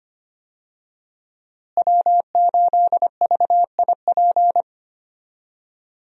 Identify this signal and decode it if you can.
Morse code, heard as a keyed tone, about 25 wpm, 700 Hz. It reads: W8VIP
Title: USA Call Signs - 5 Characters - Repeated 2x - 25wpm